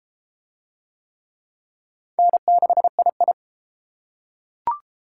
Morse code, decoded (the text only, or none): N6II